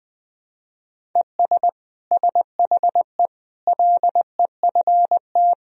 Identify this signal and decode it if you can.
Morse code, heard as a keyed tone, about 19 wpm, 700 Hz. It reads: ES SHE LEFT
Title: Sentences from Top 300 Words and Abbreviations 20wpm